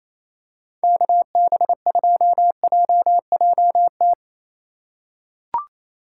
Morse code, heard as a keyed tone, about 28 wpm, 700 Hz. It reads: KB2JJT
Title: USA Call Signs - 6 Characters - 4x Character Spacing - 28wpm